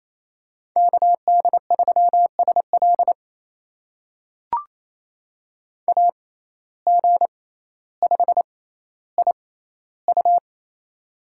Morse code, heard as a keyed tone, about 28 wpm, spaced as for 14 wpm, 700 Hz. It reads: KD3SL AG5IU
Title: USA Call Signs - 5 Characters - 6x Character Spacing - 28wpm